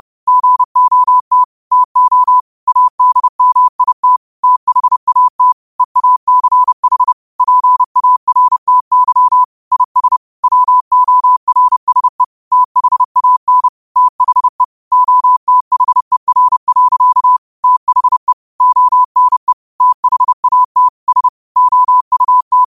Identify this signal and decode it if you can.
Morse code, heard as a keyed tone, about 30 wpm, 1000 Hz. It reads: GOT TO ADMIT THAT EACH PARTY IS WORSE THAN THE OTHER. THE ONE THAT S OUT